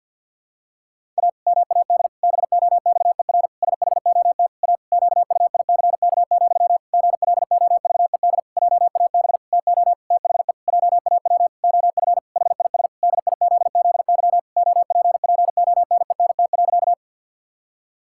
Morse code, heard as a keyed tone, about 52 wpm, 700 Hz. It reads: A MAD BOXER SHOT A QUICK, GLOVED JAB TO THE JAW OF HIS DIZZY OPPONENT.
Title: Pangram Challenge v2 - 50wpm